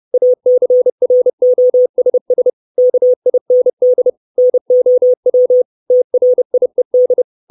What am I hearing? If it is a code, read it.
Morse code, heard as a keyed tone, about 30 wpm, 500 Hz. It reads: ACROSS KIND NOW TRIED
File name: 632 Words 30WPM